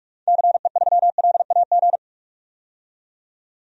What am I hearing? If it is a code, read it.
Morse code, heard as a keyed tone, about 47 wpm, 700 Hz. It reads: CE3LAG
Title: DXCC-100 Call Signs - 6 Characters - 45wpm